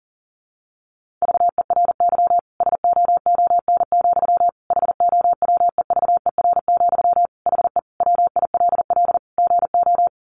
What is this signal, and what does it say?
Morse code, heard as a keyed tone, about 37 wpm, 700 Hz. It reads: VERY SOON, HOWEVER, HE WILL GO
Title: Sentences from Top 1400 Words 40wpm